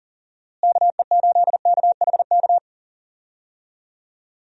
Morse code, heard as a keyed tone, about 40 wpm, 700 Hz. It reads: KE8KHK
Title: USA Call Signs - 6 Characters - Repeated 3x - 40wpm